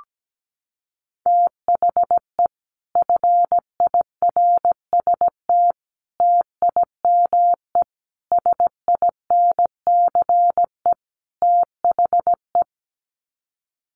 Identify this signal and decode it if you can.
Morse code, heard as a keyed tone, about 17 wpm, 700 Hz. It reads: THE FIRST TIME SINCE THE
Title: Sets of 5 Words Encore 17wpm